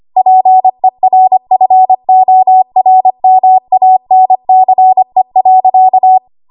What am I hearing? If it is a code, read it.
Morse code, heard as a keyed tone, about 25 wpm, 750 Hz. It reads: PERFORMANCE.
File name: Archive / 25 WPM / 25WPM